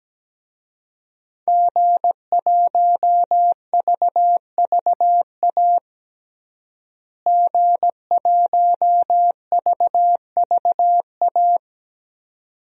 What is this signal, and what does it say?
Morse code, heard as a keyed tone, about 17 wpm, 700 Hz. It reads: G1VVA G1VVA
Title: DXCC-100 Call Signs - 5 Characters - Repeated 2x - 17wpm